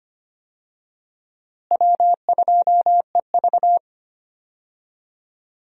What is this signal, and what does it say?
Morse code, heard as a keyed tone, about 25 wpm, 700 Hz. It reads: W2EV
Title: USA Call Signs - 4 Characters - Repeated 2x - 25wpm